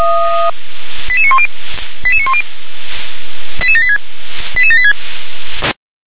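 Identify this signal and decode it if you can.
Five-tone selective call (ZVEI), heard as quick runs of five tones, tones of about 70 ms. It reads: 88910 88910 88786 88786
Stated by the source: Five-tone ZVEI selective call received on an SDR.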